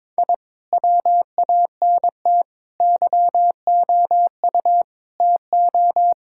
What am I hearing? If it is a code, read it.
Morse code, heard as a keyed tone, about 22 wpm, 700 Hz. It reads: I WANT YOU TO
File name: practice / 22wpm / Sentences with 4-Word Phrases - 22wpm